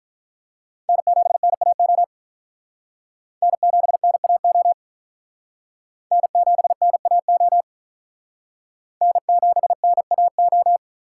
Morse code, heard as a keyed tone, about 43 wpm, 700 Hz. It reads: N7NAO N7NAO N7NAO N7NAO
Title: Speed Racing - USA 1x3 Call Signs 35wpm